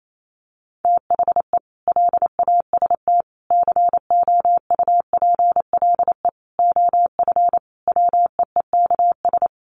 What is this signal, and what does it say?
Morse code, heard as a keyed tone, about 28 wpm, 700 Hz. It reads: THE LAST COUPLE OF WEEKS
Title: Sets of 5 Words Encore2 28wpm